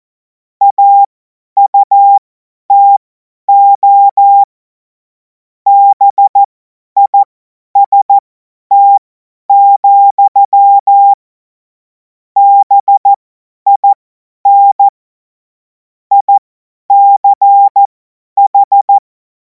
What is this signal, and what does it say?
Morse code, heard as a keyed tone, about 14 wpm, spaced as for 10 wpm, 800 Hz. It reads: AUTO BIST, BIN ICH
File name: Morsekurs / Witze / Witz 30. Juni 2026 50 F70